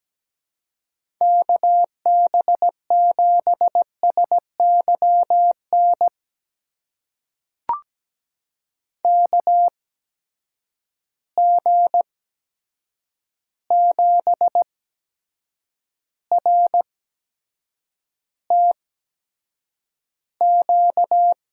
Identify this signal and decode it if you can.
Morse code, heard as a keyed tone, about 17 wpm, spaced as for 7 wpm, 700 Hz. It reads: KB7SYN KG7RTQ
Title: USA Call Signs - 6 Characters - 8x Character Spacing - 17wpm